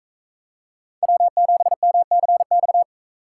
Morse code, heard as a keyed tone, about 39 wpm, 700 Hz. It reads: W7MCX